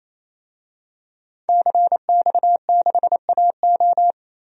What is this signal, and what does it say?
Morse code, heard as a keyed tone, about 28 wpm, 700 Hz. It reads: CX6AO
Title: DXCC-100 Call Signs - 5 Characters - Repeated 3x - 28wpm